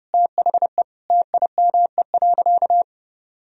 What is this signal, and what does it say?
Morse code, heard as a keyed tone, about 30 wpm, 700 Hz. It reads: THE TIME.
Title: Sentences from Top 1000 Words - Review Part 2 - 30wpm